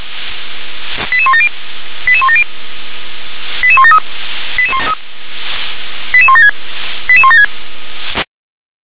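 Five-tone selective call (ZVEI), heard as quick runs of five tones, tones of about 70 ms. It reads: 88170 88170 88173 88176 88176
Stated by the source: ZVEI five-tone selective call received on an SDR.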